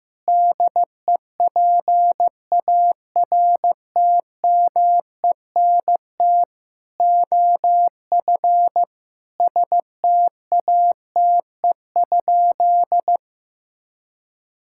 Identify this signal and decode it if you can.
Morse code, heard as a keyed tone, about 16 wpm, 700 Hz. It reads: DEPARTMENT OF STATE?
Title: Sentences from Top 1000 Words - Review Part 5 - 15wpm